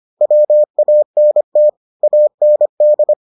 Morse code, heard as a keyed tone, about 25 wpm, 600 Hz. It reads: WANT AND